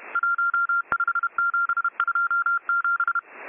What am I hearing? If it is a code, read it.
Morse code, heard as a keyed tone, about 31 wpm, 1390 Hz. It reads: 15727